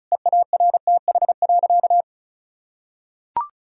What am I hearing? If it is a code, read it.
Morse code, heard as a keyed tone, about 34 wpm, 700 Hz. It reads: EARTH.
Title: Sentences from Top 1000 Words - Review Part 2 - 35wpm